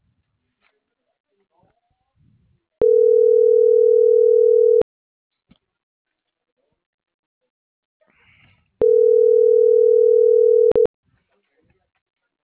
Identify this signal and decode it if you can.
Ringback tone (North America), 440 + 480 Hz.